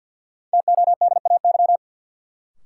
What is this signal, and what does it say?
Morse code, heard as a keyed tone, about 51 wpm, 700 Hz. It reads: TODAY
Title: Sets of 5 Words Encore - New Word Drill - 50wpm